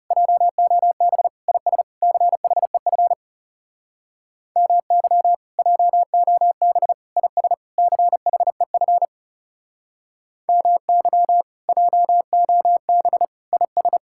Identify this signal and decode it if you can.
Morse code, heard as a keyed tone, about 36 wpm, 700 Hz. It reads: JOB IS CHEF MY JOB IS CHEF MY JOB IS
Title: Speed-Racing Common QSO Phrases - 30wpm